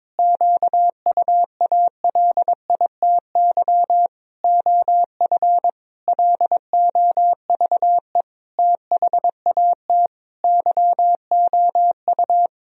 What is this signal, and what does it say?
Morse code, heard as a keyed tone, about 22 wpm, 700 Hz. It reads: QUALITY OF LOVE THAT YOU
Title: Sentences from Top 1000 Words - Review Part 4 - 22wpm